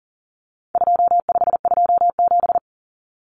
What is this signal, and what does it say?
Morse code, heard as a keyed tone, about 40 wpm, 700 Hz. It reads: 2527
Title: Numbers - 4-Digit 40wpm